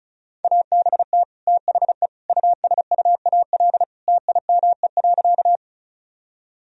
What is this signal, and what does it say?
Morse code, heard as a keyed tone, about 36 wpm, 700 Hz. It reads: ABT THE USUAL TIME.
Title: Sentences from Top 400 Words and Abbreviations 35wpm